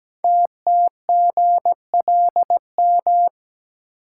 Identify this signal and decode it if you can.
Morse code, heard as a keyed tone, about 17 wpm, 700 Hz. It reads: TTGLM
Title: Code Groups - 5 Letters - 17wpm